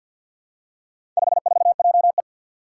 Morse code, heard as a keyed tone, about 51 wpm, 700 Hz. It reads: 54JE
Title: Four Challenging Letter-Number Combinations 50wpm